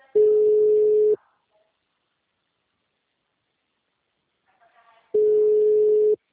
Ringback tone (Europe, 425 Hz).